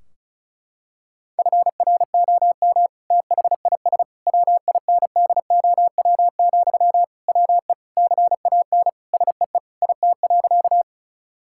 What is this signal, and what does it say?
Morse code, heard as a keyed tone, about 36 wpm, 700 Hz. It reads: FROM THIS WINDOW, WE CAN SEE IT.
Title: Sentences from Top 1600 Words 35wpm